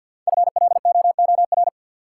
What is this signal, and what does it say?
Morse code, heard as a keyed tone, about 52 wpm, 700 Hz. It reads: FLOOR
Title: Five Letter Words - 50wpm